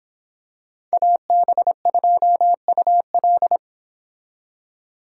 Morse code, heard as a keyed tone, about 26 wpm, 700 Hz. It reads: AB2UL